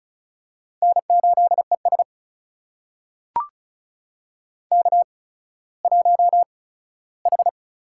Morse code, heard as a keyed tone, about 36 wpm, spaced as for 17 wpm, 700 Hz. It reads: N8ES K1H
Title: USA Call Signs - 4 Characters - 8x Character Spacing - 35wpm